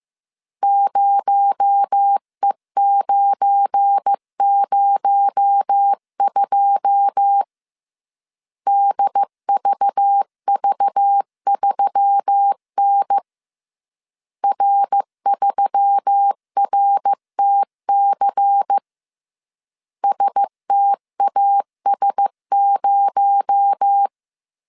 Morse code, heard as a keyed tone, about 15 wpm, 800 Hz. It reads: 0E902 DVV3N R3RTC STAS0
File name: CW / DET / 1 9 1500C S A 14wpm 3222